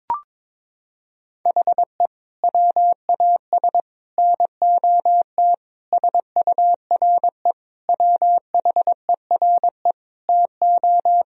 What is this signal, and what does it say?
Morse code, heard as a keyed tone, about 22 wpm, 700 Hz. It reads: HE WAS NOT SURE WHERE TO
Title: Sentences from Top 1000 Words - Review Part 1 - 22wpm